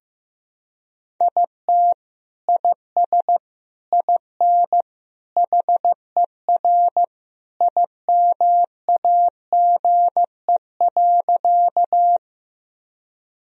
Morse code, heard as a keyed tone, about 15 wpm, 700 Hz. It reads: IT IS IN HER IMAGE.
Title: Sentences from Top 1000 Words - Review Part 4 - 15wpm